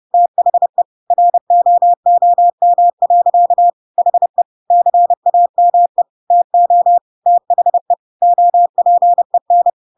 Morse code, heard as a keyed tone, about 30 wpm, 700 Hz. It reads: THE ROOM. HE CAME TO THE OPEN
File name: Pinocchio / Pinocchio ch3 30 30 ik1hge